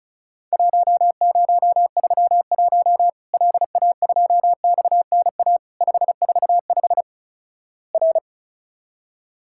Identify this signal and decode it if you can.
Morse code, heard as a keyed tone, about 36 wpm, 700 Hz. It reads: 1031 LA2XNA 545 R
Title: Worked All Europe DX Contest - QTC Exchange - 35wpm